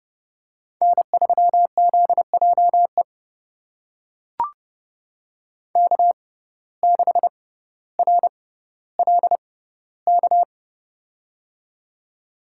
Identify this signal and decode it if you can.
Morse code, heard as a keyed tone, about 30 wpm, spaced as for 15 wpm, 700 Hz. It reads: N3ZJE K6RLK
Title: USA Call Signs - 5 Characters - 6x Character Spacing - 30wpm